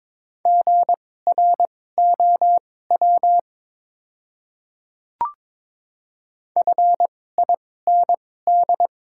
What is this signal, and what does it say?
Morse code, heard as a keyed tone, about 22 wpm, spaced as for 16 wpm, 700 Hz. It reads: GROW FIND
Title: 2x Character Spacing - Top 500 Words - 22wpm